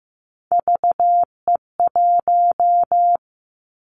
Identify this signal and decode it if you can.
Morse code, heard as a keyed tone, about 15 wpm, 700 Hz. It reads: VE1
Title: Callsign Prefixes for Top 100 DXCC - 15wpm